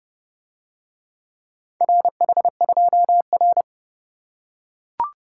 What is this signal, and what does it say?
Morse code, heard as a keyed tone, about 30 wpm, 700 Hz. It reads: RH2R